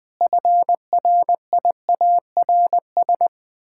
Morse code, heard as a keyed tone, about 20 wpm, 700 Hz. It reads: FRIARS
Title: Pangram Challenge v2 - 20wpm